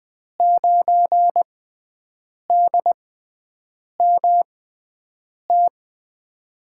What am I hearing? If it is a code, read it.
Morse code, heard as a keyed tone, about 21 wpm, spaced as for 7 wpm, 700 Hz. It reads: 9DMT
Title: USA Call Signs - 5 Characters - 6x Character Spacing - 20wpm